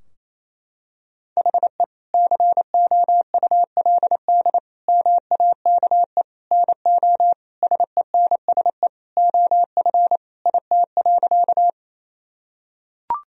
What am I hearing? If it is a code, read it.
Morse code, heard as a keyed tone, about 28 wpm, 700 Hz. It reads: HE COULD MAKE NO SENSE OF IT.
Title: Sentences from Top 1000 Words - Review Part 4 - 28wpm